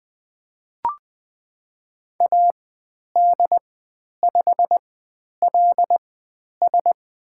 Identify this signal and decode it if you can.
Morse code, heard as a keyed tone, about 20 wpm, spaced as for 10 wpm, 700 Hz. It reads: AD5LS
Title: 109 Lesson 18g - Letter m - Call Signs 20wpm